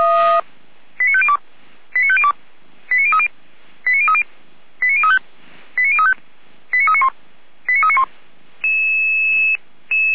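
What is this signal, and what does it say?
Five-tone selective call (ZVEI), heard as quick runs of five tones, tones of about 70 ms. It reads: 89502 89502 89030 89030 89035 89035 89391 89391
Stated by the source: ZVEI five-tone call 89042 received by SDR.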